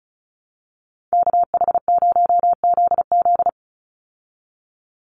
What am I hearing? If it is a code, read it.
Morse code, heard as a keyed tone, about 34 wpm, 700 Hz. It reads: KH0ZZ
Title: Call Signs Repeated 2x - 35wpm